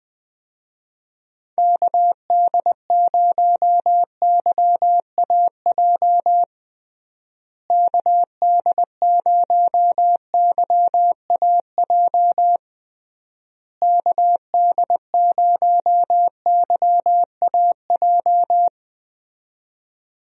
Morse code, heard as a keyed tone, about 19 wpm, 700 Hz. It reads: KD0YAJ KD0YAJ KD0YAJ
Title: USA Call Signs - 6 Characters - Repeated 3x - 20wpm